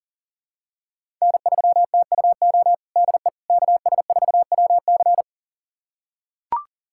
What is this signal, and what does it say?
Morse code, heard as a keyed tone, about 40 wpm, 700 Hz. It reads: N3TUO DE KS4WC
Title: USA Callsign DE Callsign - 5 Characters - 40wpm